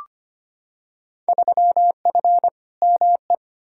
Morse code, heard as a keyed tone, about 25 wpm, 700 Hz. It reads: 3F ME
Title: Field Day Exchanges - 25wpm